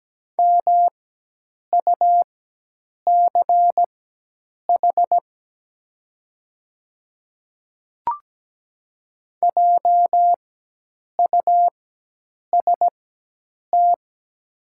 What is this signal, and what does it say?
Morse code, heard as a keyed tone, about 17 wpm, spaced as for 8 wpm, 700 Hz. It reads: MUCH JUST
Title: 4x Character Spacing - Top 200 Words - 17wpm